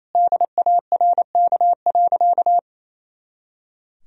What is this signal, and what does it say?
Morse code, heard as a keyed tone, about 28 wpm, 700 Hz. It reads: DARK.
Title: Sentences from Top 500 Words and Abbreviations 28wpm